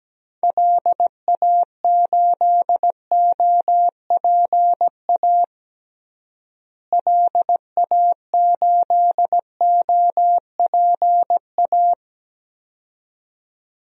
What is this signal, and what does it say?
Morse code, heard as a keyed tone, about 17 wpm, 700 Hz. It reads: LA8OPA LA8OPA
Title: DXCC-100 Call Signs - 6 Characters - Repeated 2x - 17wpm